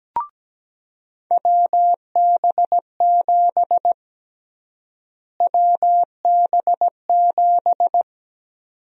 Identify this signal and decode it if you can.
Morse code, heard as a keyed tone, about 17 wpm, 700 Hz. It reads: WB7 WB7
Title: USA Call Sign Prefixes - Repeated 2x - 17wpm